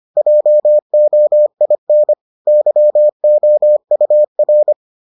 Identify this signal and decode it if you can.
Morse code, heard as a keyed tone, about 25 wpm, 600 Hz. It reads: JOIN YOUR